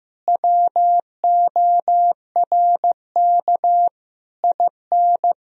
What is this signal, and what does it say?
Morse code, heard as a keyed tone, about 15 wpm, 700 Hz. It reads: WORK IN
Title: Sentences from Top 1000 Words - Review Part 4 - 15wpm